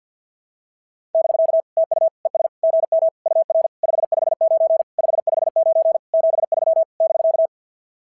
Morse code, heard as a keyed tone, about 46 wpm, 660 Hz. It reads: <BK> TU ES GM UR 559 559 73 <BK>
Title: SOTA QSO Practice 50wpm